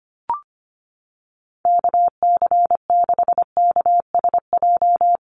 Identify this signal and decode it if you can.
Morse code, heard as a keyed tone, about 25 wpm, 700 Hz. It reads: KC6KSJ
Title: USA Call Signs Repeated 3x 25wpm